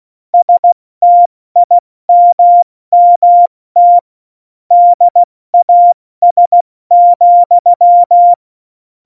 Morse code, heard as a keyed tone, about 16 wpm, spaced as for 14 wpm, 700 Hz. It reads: STIMMT DAS,
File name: Morsekurs / Witze / Witz 16. Juli 2019, 70 F80